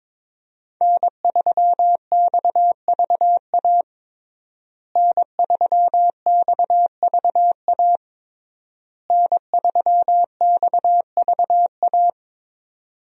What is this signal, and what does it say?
Morse code, heard as a keyed tone, about 22 wpm, 700 Hz. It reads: N3XVA N3XVA N3XVA
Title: USA Call Signs - 5 Characters - Repeated 3x - 22wpm